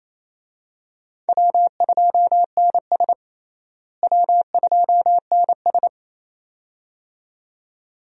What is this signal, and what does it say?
Morse code, heard as a keyed tone, about 28 wpm, 700 Hz. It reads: W2NS W2NS
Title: USA Call Signs - 4 Characters - Repeated 2x - 28wpm